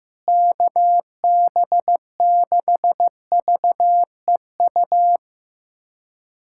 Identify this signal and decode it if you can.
Morse code, heard as a keyed tone, about 15 wpm, 700 Hz. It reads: KB6VEU
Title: USA Callsign DE Callsign - 6 Characters - 15wpm